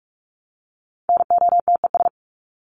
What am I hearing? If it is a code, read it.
Morse code, heard as a keyed tone, about 44 wpm, 700 Hz. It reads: NOTES
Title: ES:S Suffix - Top 100 Words with Base Words 45wpm